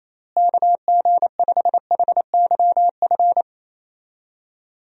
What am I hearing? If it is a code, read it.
Morse code, heard as a keyed tone, about 28 wpm, 700 Hz. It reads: KG5HYF
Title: USA Call Signs - 6 Characters - Repeated 3x - 28wpm